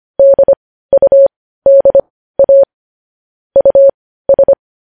Morse code, heard as a keyed tone, about 25 wpm, spaced as for 15 wpm, 550 Hz. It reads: DUDA US
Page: news headlines in morse code at 15 wpm